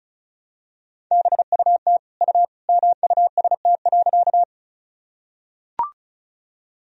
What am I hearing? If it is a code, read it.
Morse code, heard as a keyed tone, about 34 wpm, 700 Hz. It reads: BUT U MUST.
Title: Sentences from Top 200 Words and Abbreviations 35wpm